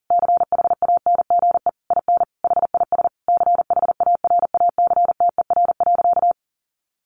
Morse code, heard as a keyed tone, about 40 wpm, 700 Hz. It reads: CHANGE IN HIS CHARACTER.
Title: Sentences from Top 1200 Words 40wpm